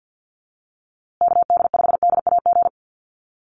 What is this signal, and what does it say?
Morse code, heard as a keyed tone, about 51 wpm, 700 Hz. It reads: KD5DAG
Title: USA Call Signs Repeated 2x 50wpm